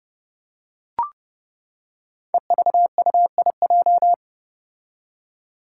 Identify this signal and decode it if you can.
Morse code, heard as a keyed tone, about 30 wpm, 700 Hz. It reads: EVUIJ
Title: Five Challenging Letter-Number Combinations 30wpm